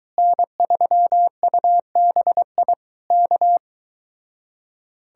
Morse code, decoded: N3UBI K

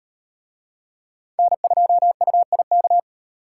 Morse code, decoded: N2UIK

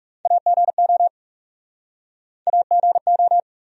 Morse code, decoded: AGO AGO

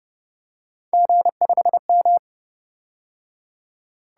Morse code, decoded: G5M

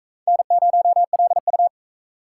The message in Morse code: N0LU

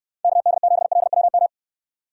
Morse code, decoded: KD7LWM

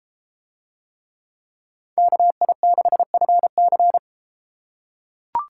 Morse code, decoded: KI6FC

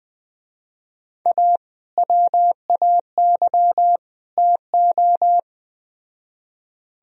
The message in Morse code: A WAY TO

